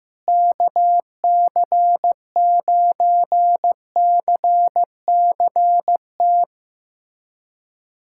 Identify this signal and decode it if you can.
Morse code, heard as a keyed tone, about 16 wpm, 700 Hz. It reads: KC9CCT
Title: USA Callsign DE Callsign - 6 Characters - 15wpm